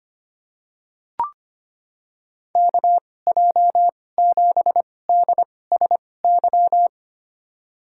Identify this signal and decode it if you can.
Morse code, heard as a keyed tone, about 25 wpm, spaced as for 18 wpm, 700 Hz. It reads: KJ7DSY
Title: USA Call Signs - 2x Character Spacing - 25wpm